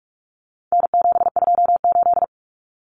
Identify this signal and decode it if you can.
Morse code, heard as a keyed tone, about 44 wpm, 700 Hz. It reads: N728
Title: Common Cut Numbers - 4-Digit 45wpm